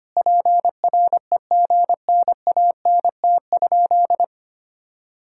Morse code, heard as a keyed tone, about 25 wpm, 700 Hz. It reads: PREGNANT?
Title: Sentences from Top 1000 Words - Review Part 6 - 25wpm